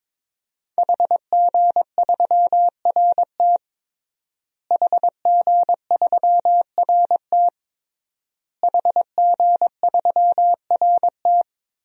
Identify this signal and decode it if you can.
Morse code, heard as a keyed tone, about 22 wpm, 700 Hz. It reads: HG3RT HG3RT HG3RT
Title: DXCC-100 Call Signs - 5 Characters - Repeated 3x - 22wpm